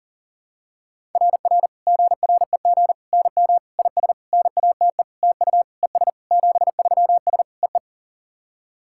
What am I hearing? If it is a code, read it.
Morse code, heard as a keyed tone, about 40 wpm, 700 Hz. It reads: RR GREG NM IS NATE TU ES 73S EE
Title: SOTA QSO Practice 40wpm